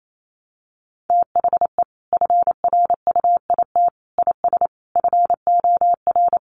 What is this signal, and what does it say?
Morse code, heard as a keyed tone, about 28 wpm, 700 Hz. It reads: THE FRUIT IS FOR
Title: Sentences from Top 1300 Words 28wpm